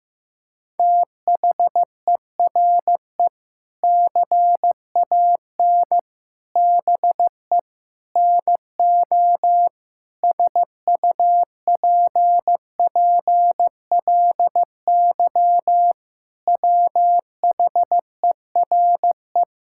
Morse code, heard as a keyed tone, about 15 wpm, 700 Hz. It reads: THERE CAN BE NO SUPPLY WHERE
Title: Sentences from Top 1000 Words - Review Part 5 - 15wpm